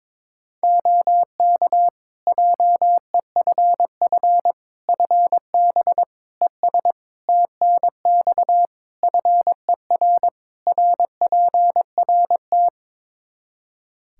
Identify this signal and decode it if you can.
Morse code, heard as a keyed tone, about 22 wpm, 700 Hz. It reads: OK JEFF FB ES TNX FER RPRT
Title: K1USN SST QSO Practice 22wpm